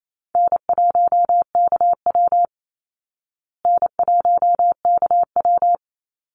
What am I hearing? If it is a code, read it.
Morse code, heard as a keyed tone, about 28 wpm, 700 Hz. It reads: N1KW N1KW
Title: Easy Call Signs Repeated 3x - 28wpm